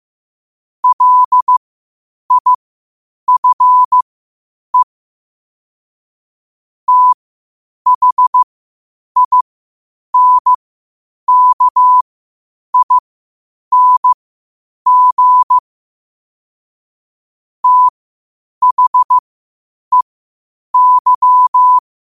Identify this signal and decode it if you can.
Morse code, heard as a keyed tone, about 15 wpm, spaced as for 8 wpm, 1000 Hz. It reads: LIFE THINKING THEY
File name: 8wpm